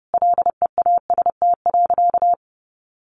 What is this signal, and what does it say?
Morse code, heard as a keyed tone, about 30 wpm, 700 Hz. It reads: LEAST.
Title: Sentences from Top 1200 Words 30wpm